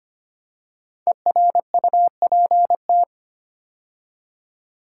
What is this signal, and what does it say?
Morse code, heard as a keyed tone, about 26 wpm, 700 Hz. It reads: ERUPT
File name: 5 Letter Words - First 3 Letters Given - 25wpm